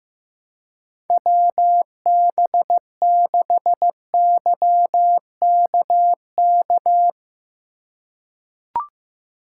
Morse code, heard as a keyed tone, about 15 wpm, 700 Hz. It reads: WB6YKK
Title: USA Call Signs - 2x Character Spacing - 15wpm